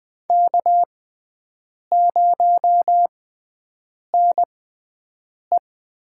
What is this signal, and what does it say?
Morse code, heard as a keyed tone, about 20 wpm, spaced as for 7 wpm, 700 Hz. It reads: K0NE
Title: USA Call Signs - 4 Characters - 6x Character Spacing - 20wpm